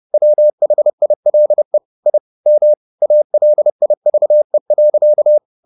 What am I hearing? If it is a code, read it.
Morse code, heard as a keyed tone, about 30 wpm, 600 Hz. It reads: WHILE I M ALIVE.
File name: the old man and the sea / the old man and the sea 1 30wpm